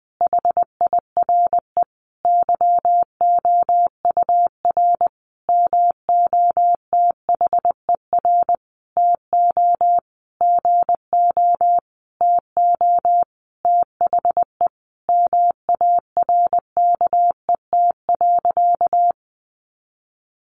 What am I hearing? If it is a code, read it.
Morse code, heard as a keyed tone, about 19 wpm, 700 Hz. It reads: HIRE YOUR MOTHER TO GO TO THE MARKET.
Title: Sentences from Top 1400 Words 20wpm